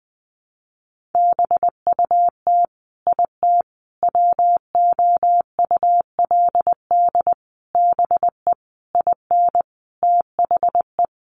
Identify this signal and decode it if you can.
Morse code, heard as a keyed tone, about 20 wpm, 700 Hz. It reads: BUT IT WOULD BE IN THE
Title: Sentences from Top 300 Words 20wpm